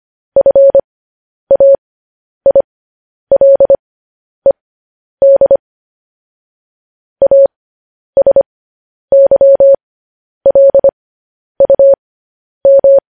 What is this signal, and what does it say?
Morse code, heard as a keyed tone, about 25 wpm, spaced as for 10 wpm, 550 Hz. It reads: FAILED ASYLUM
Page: News Headlines in Morse Code at 10 WPM